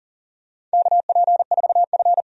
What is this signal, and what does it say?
Morse code, heard as a keyed tone, about 40 wpm, 700 Hz. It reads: KP4F
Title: DXCC-100 Call Signs - 4 Characters - 40wpm